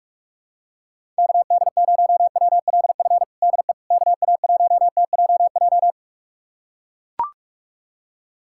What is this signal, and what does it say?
Morse code, heard as a keyed tone, about 44 wpm, 700 Hz. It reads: KD0WLF DE KA1TJJ
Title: USA Callsign DE Callsign - 6 Characters - 45wpm